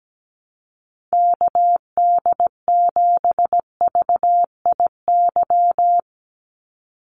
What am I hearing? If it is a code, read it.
Morse code, heard as a keyed tone, about 17 wpm, 700 Hz. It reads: KD7VIY
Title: USA Call Signs 17wpm